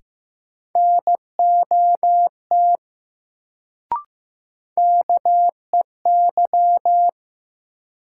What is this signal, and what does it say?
Morse code, heard as a keyed tone, about 15 wpm, 700 Hz. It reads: NOT KEY
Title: Sentences from Top 600 Words - New Word Drill - 15wpm